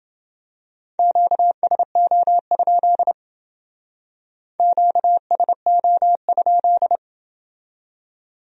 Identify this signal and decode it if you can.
Morse code, heard as a keyed tone, about 29 wpm, 700 Hz. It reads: QSO? QSO?